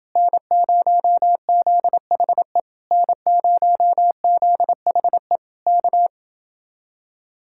Morse code, heard as a keyed tone, about 29 wpm, 700 Hz. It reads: N0ZHE N0ZHE K